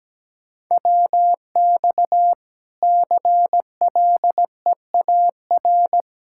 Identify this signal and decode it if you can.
Morse code, heard as a keyed tone, about 17 wpm, 700 Hz. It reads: WX CLEAR
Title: Practice QSO - Exchange 2 - Info - 17wpm